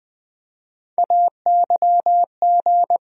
Morse code, heard as a keyed tone, about 21 wpm, 700 Hz. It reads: AYG
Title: USA Call Sign Postfixes - 3 Letters 20wpm